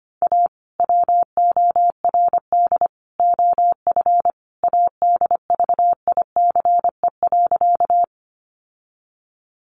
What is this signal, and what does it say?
Morse code, heard as a keyed tone, about 25 wpm, 700 Hz. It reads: A WORD OF ADVICE.
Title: Sentences from Top 1100 Words 25wpm